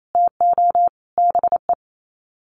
Morse code, heard as a keyed tone, about 28 wpm, 700 Hz. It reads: TO BE
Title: Sets of 5 Words Encore2 28wpm